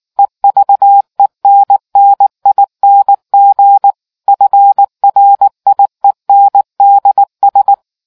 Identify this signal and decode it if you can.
Morse code, heard as a keyed tone, about 19 wpm, 800 Hz. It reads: EVENNING FRIENDS